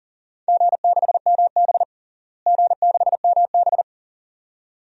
Morse code, heard as a keyed tone, about 40 wpm, 700 Hz. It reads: G6MB G6MB
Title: DXCC-100 Call Signs - 4 Characters - Repeated 2x - 40wpm